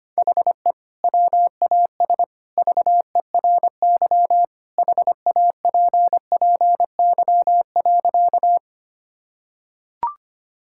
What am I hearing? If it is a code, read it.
Morse code, heard as a keyed tone, about 25 wpm, 700 Hz. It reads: HE WAS VERY HAPPY.
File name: practice / 25wpm / Sentences with 2-Word Phrases - 25wpm